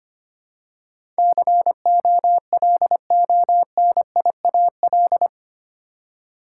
Morse code, heard as a keyed tone, about 25 wpm, 700 Hz. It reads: COLONIAL
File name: Eight Letter Words - 25wpm